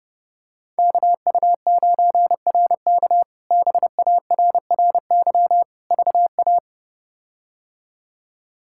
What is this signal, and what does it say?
Morse code, heard as a keyed tone, about 30 wpm, 700 Hz. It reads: KU9RK BARRY VA
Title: K1USN SST QSO Practice - 30wpm